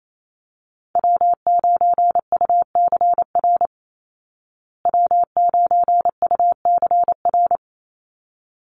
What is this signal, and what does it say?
Morse code, heard as a keyed tone, about 28 wpm, 700 Hz. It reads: W9UCR W9UCR
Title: Call Signs Repeated 3x - 28wpm